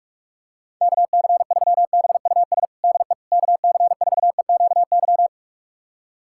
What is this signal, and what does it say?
Morse code, heard as a keyed tone, about 43 wpm, 700 Hz. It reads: KC3BUS DE KC4EQY
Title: USA Callsign DE Callsign 45wpm